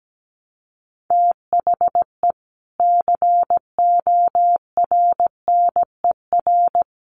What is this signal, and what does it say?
Morse code, heard as a keyed tone, about 17 wpm, 700 Hz. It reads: THE CORNER
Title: Sets of 5 Words Encore2 17wpm